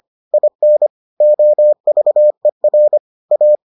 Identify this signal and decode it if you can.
Morse code, heard as a keyed tone, about 25 wpm, 600 Hz. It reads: IN OVER A